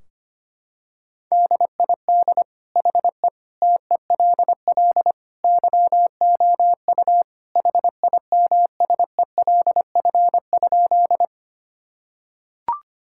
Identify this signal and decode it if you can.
Morse code, heard as a keyed tone, about 25 wpm, 700 Hz. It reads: DID HE TELL YOU HIMSELF?
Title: Sentences from Top 1600 Words 25wpm